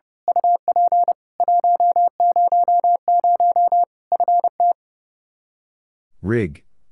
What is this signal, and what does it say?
Morse code, heard as a keyed tone, about 30 wpm, 700 Hz. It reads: UP 100 FT
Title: Practice QSO - Exchange 2 - Info - 30wpm